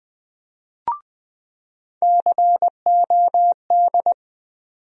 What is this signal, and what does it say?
Morse code, heard as a keyed tone, about 20 wpm, 700 Hz. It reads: COD
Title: USA Call Sign Postfixes - 3 Letters 20wpm